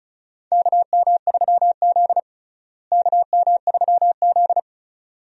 Morse code, heard as a keyed tone, about 34 wpm, 700 Hz. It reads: KM3Z KM3Z
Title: USA Call Signs - 4 Characters - Repeated 2x - 35wpm